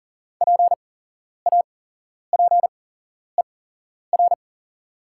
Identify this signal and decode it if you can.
Morse code, heard as a keyed tone, about 40 wpm, spaced as for 11 wpm, 700 Hz. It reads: PAPER
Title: 8x Character Spacing - Top 300 Words - 40wpm